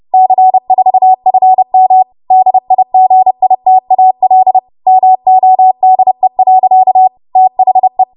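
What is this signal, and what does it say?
Morse code, heard as a keyed tone, about 30 wpm, 750 Hz. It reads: C4FM DIGITAL MODE. THE
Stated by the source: Morse practice transmission at 30 WPM.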